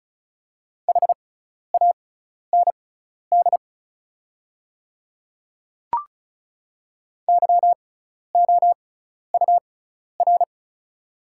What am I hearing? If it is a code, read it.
Morse code, heard as a keyed tone, about 36 wpm, spaced as for 12 wpm, 700 Hz. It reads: HAND YOUR
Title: 6x Character Spacing - Top 300 Words - 35wpm